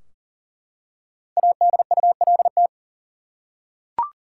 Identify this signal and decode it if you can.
Morse code, heard as a keyed tone, about 40 wpm, 700 Hz. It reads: ADULT
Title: Five Letter Words - 40wpm